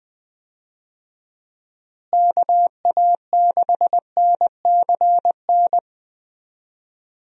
Morse code, heard as a keyed tone, about 20 wpm, 700 Hz. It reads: KA6NCN
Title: USA Call Signs - 6 Characters - Repeated 2x - 20wpm